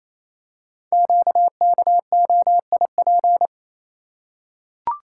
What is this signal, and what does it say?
Morse code, heard as a keyed tone, about 27 wpm, 700 Hz. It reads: QKOIP
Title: Code Groups - 5 Letters - 28wpm